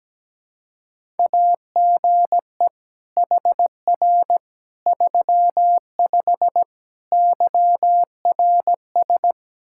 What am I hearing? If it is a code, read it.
Morse code, heard as a keyed tone, about 17 wpm, 700 Hz. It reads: AGE HR 35 YRS
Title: Practice QSO - Exchange 3 - More Info - 17wpm